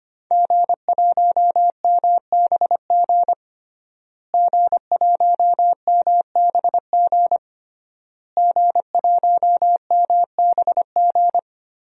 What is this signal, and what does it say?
Morse code, heard as a keyed tone, about 25 wpm, 700 Hz. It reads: G1MBG G1MBG G1MBG
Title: DXCC-100 Call Signs - 5 Characters - Repeated 3x - 25wpm